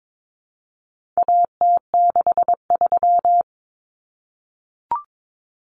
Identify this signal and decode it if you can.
Morse code, heard as a keyed tone, about 22 wpm, 700 Hz. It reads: AT63
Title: Common Cut Numbers - 4-Digit 22wpm